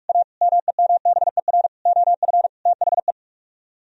Morse code, heard as a keyed tone, about 44 wpm, 700 Hz. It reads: A MEMBER OF THE